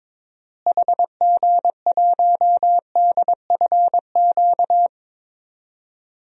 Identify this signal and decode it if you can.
Morse code, heard as a keyed tone, about 22 wpm, 700 Hz. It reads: HG1DFQ
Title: DXCC-100 Call Signs - 6 Characters - Repeated 3x - 22wpm